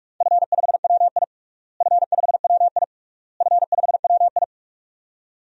Morse code, heard as a keyed tone, about 47 wpm, 700 Hz. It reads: F5WI F5WI F5WI
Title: DXCC-100 Call Signs - 4 Characters - Repeated 3x - 45wpm